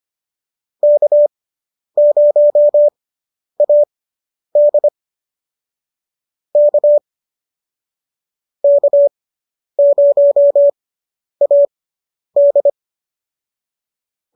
Morse code, heard as a keyed tone, about 25 wpm, spaced as for 10 wpm, 590 Hz. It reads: K0AD K K0AD